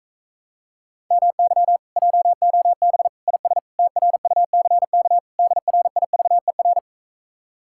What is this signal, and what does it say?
Morse code, heard as a keyed tone, about 46 wpm, 700 Hz. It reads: MY JOB IS TRUCK DRIVER